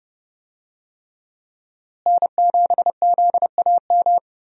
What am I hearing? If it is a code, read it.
Morse code, heard as a keyed tone, about 31 wpm, 700 Hz. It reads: N7ZAM
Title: USA Call Signs - 5 Characters - Repeated 3x - 30wpm